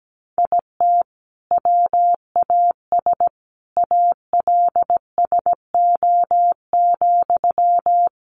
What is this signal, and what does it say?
Morse code, heard as a keyed tone, about 17 wpm, 700 Hz. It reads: IT WAS ALSO,
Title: Sentences from Top 1400 Words 17wpm